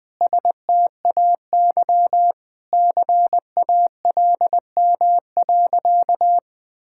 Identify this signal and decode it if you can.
Morse code, heard as a keyed tone, about 20 wpm, 700 Hz. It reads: STAY CALM.
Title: Sentences with 7-Word Phrases - 20wpm